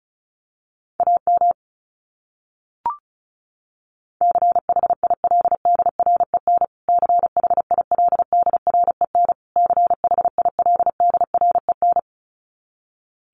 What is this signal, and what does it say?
Morse code, heard as a keyed tone, about 34 wpm, 700 Hz. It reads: AM CHILDREN CHILDREN CHILDREN
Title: Top 500 Words Repeated 35wpm